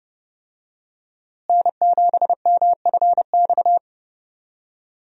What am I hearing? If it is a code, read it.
Morse code, heard as a keyed tone, about 32 wpm, 700 Hz. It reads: N7MFX
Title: USA Call Signs - 5 Characters - Repeated 2x - 30wpm